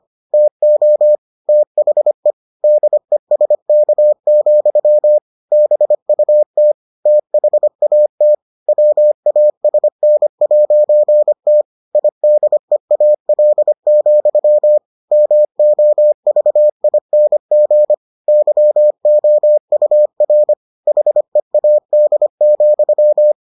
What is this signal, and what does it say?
Morse code, heard as a keyed tone, about 25 wpm, 600 Hz. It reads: TO THE DESK, BUT THAT WASN'T IDEAL, MOVING YOUR HEAD,